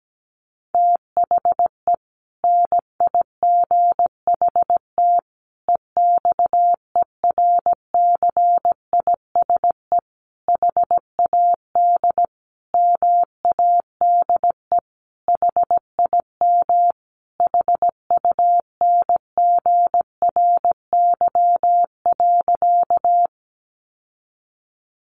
Morse code, heard as a keyed tone, about 17 wpm, 700 Hz. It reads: THE NIGHT EXERCISE HAD MADE HIM HUNGRY.
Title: Sentences from Top 1400 Words 17wpm